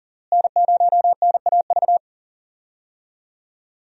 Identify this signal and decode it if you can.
Morse code, heard as a keyed tone, about 37 wpm, 700 Hz. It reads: N0NAV
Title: USA Call Signs - 5 Characters - Repeated 3x - 40wpm